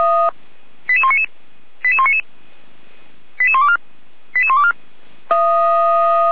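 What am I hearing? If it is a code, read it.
Five-tone selective call (ZVEI), heard as quick runs of five tones, tones of about 70 ms. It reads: 88199 88199 88124 88124
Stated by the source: ZVEI five-tone call received by SDR.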